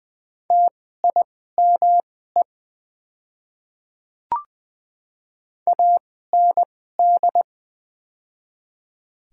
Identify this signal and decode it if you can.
Morse code, heard as a keyed tone, about 20 wpm, spaced as for 15 wpm, 700 Hz. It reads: TIME AND
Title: 2x Character Spacing - Top 100 Words - 20wpm